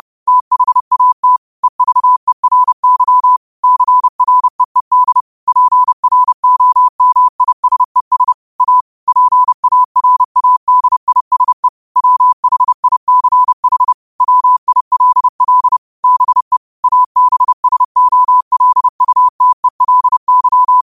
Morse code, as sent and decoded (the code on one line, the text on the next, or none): - .... .- - / . ...- . .-. -.-- / -.-. .-. . -.. / .--. .-. --- -- .. ... ... / .- / .--. .- .-. .- -.. .. ... / .-- .... .. -.-. .... / .-- .. .-.. .-.. / -... / .- -... ... --- .-.. ..- - . .-.. -.--
THAT EVERY CREED PROMISES A PARADISE WHICH WILL BE ABSOLUTELY